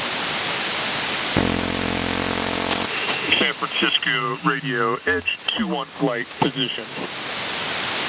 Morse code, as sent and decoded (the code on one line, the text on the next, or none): none